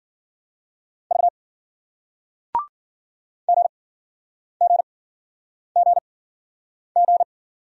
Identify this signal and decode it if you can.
Morse code, heard as a keyed tone, about 56 wpm, spaced as for 9 wpm, 700 Hz. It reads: V GGGG